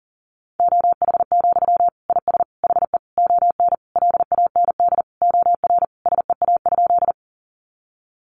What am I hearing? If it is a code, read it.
Morse code, heard as a keyed tone, about 40 wpm, 700 Hz. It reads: OH, IS HE ON LAND OR SEA?
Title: Sentences from Top 500 Words 40wpm